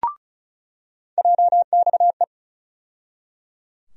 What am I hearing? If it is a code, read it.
Morse code, heard as a keyed tone, about 34 wpm, 700 Hz. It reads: JXE
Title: USA Call Sign Postfixes - 3 Letters 35wpm